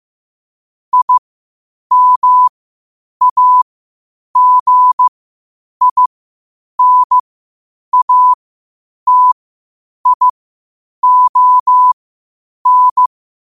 Morse code, heard as a keyed tone, about 15 wpm, spaced as for 8 wpm, 1000 Hz. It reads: IMAGINATION
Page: Quotes for Fri, 15 Aug 2025 in Morse Code at 8 words per minute.